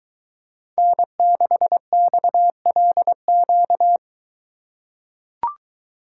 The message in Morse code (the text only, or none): N6XLQ